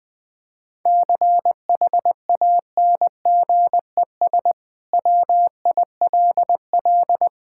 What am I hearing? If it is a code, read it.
Morse code, heard as a keyed tone, about 20 wpm, 700 Hz. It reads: CHANGES WILL
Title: Sentences with 5-Word Phrases - 20wpm